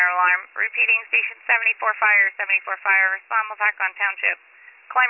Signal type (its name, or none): Emergency Alert System (SAME) tones